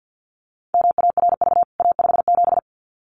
Morse code, heard as a keyed tone, about 48 wpm, 700 Hz. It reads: MARV A57
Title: CWT Exchange 50wpm